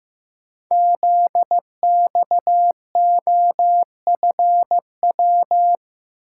Morse code, heard as a keyed tone, about 15 wpm, 700 Hz. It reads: ZXOFW
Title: Code Groups - 5 Letter:Numbers - 15wpm